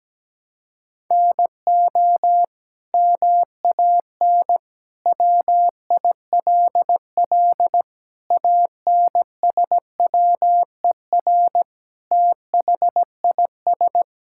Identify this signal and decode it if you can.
Morse code, heard as a keyed tone, about 17 wpm, 700 Hz. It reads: NO MAN WILL ANSWER THIS
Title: Sentences from Top 1000 Words - Review Part 6 - 17wpm